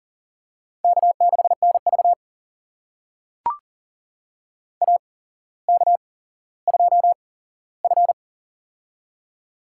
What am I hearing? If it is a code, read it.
Morse code, heard as a keyed tone, about 40 wpm, spaced as for 17 wpm, 700 Hz. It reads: K6NV AK2F
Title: USA Call Signs - 4 Characters - 8x Character Spacing - 40wpm